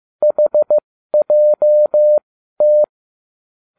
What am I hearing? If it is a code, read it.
Morse code, heard as a keyed tone, about 15 wpm, spaced as for 12 wpm, 600 Hz. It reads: HJT